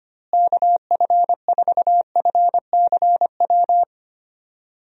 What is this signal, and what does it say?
Morse code, heard as a keyed tone, about 25 wpm, 700 Hz. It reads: KF4FCW
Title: USA Call Signs - 6 Characters - Repeated 3x - 25wpm